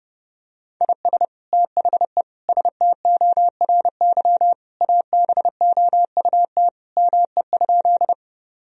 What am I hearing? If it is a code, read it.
Morse code, heard as a keyed tone, about 30 wpm, 700 Hz. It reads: IS THE STORY ABOUT ME?
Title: Sentences from Top 1000 Words - Review Part 1 - 30wpm